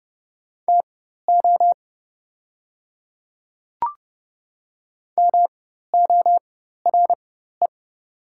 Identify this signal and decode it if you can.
Morse code, heard as a keyed tone, about 30 wpm, spaced as for 14 wpm, 690 Hz. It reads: TO MORE
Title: 4x Character Spacing - Top 100 Words - 30wpm